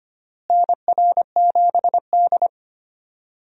Morse code, heard as a keyed tone, about 25 wpm, 700 Hz. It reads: NR7D
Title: USA Call Signs - 4 Characters - Repeated 3x - 25wpm